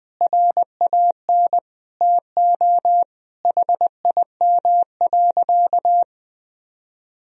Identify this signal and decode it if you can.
Morse code, heard as a keyed tone, about 20 wpm, 700 Hz. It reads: RAN TO HIM.
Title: Sentences from Top 1000 Words - Review Part 3 - 20wpm